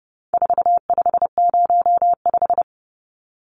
Morse code, heard as a keyed tone, about 28 wpm, 700 Hz. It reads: 4505
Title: Numbers - 4-Digit 30wpm